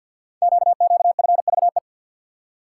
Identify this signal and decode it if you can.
Morse code, heard as a keyed tone, about 52 wpm, 690 Hz. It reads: QQFVE
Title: Code Groups - 5 Letter:Numbers - 50wpm